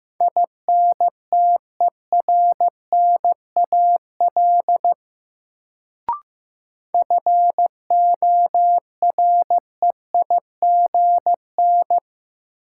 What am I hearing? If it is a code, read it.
Morse code, heard as a keyed tone, about 15 wpm, 700 Hz. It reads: INTERNAL FOREIGN
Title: Sentences from Top 1400 Words - New Word Drill Alt - 15wpm